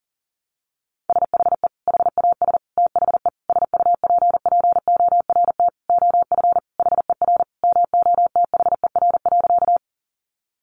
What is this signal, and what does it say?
Morse code, heard as a keyed tone, about 40 wpm, 690 Hz. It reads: SHE HAS THE SUPPORT OF HER MOTHER.
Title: Sentences from Top 800 Words 40wpm